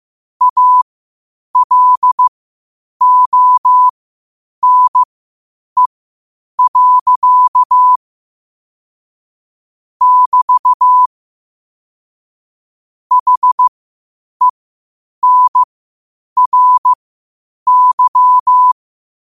Morse code, heard as a keyed tone, about 15 wpm, spaced as for 8 wpm, 1000 Hz. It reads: ALONE. <BT> HENRY
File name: qotdfiles / 8wpm